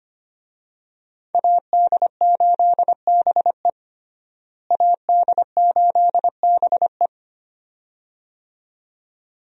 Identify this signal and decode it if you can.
Morse code, heard as a keyed tone, about 25 wpm, 700 Hz. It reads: AD8BE AD8BE
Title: USA Call Signs - 5 Characters - Repeated 2x - 25wpm